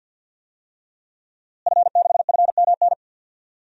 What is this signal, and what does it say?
Morse code, heard as a keyed tone, about 48 wpm, 700 Hz. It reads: F6FMN
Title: DXCC-100 Call Signs - 5 Characters - Repeated 3x - 50wpm